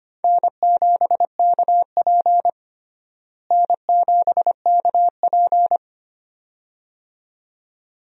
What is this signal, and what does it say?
Morse code, heard as a keyed tone, about 25 wpm, 700 Hz. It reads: N7KP N7KP
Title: USA Call Signs - 4 Characters - Repeated 2x - 25wpm